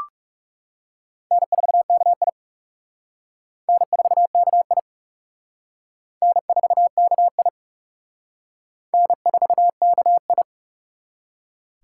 Morse code, heard as a keyed tone, about 36 wpm, 700 Hz. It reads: N4KI N4KI N4KI N4KI
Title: Speed Racing - USA 1x2 Call Signs 30wpm